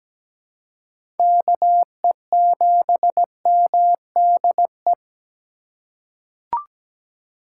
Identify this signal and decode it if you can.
Morse code, heard as a keyed tone, about 17 wpm, 700 Hz. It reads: KE7MDE
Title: USA Call Signs - 4x Character Spacing - 17wpm